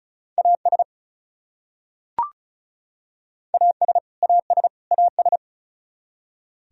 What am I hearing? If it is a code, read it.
Morse code, heard as a keyed tone, about 34 wpm, 700 Hz. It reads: AS AS AS AS
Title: Top 100 Words Repeated 35wpm